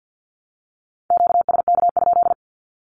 Morse code, heard as a keyed tone, about 48 wpm, 700 Hz. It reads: QSK?
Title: Q-Codes Repeated 3x 50wpm